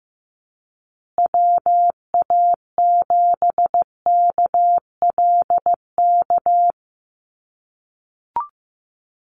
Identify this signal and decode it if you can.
Morse code, heard as a keyed tone, about 15 wpm, 700 Hz. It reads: WA7KLK
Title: USA Call Signs Repeated 3x 15wpm